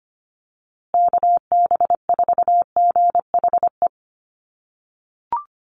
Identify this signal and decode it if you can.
Morse code, heard as a keyed tone, about 25 wpm, 700 Hz. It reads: KB4GHE
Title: USA Call Signs Repeated 3x 25wpm